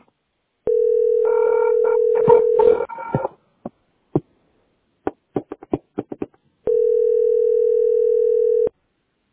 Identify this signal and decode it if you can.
Ringback tone (North America), 440 + 480 Hz.
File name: 399d2f76-4980-42a6-8abf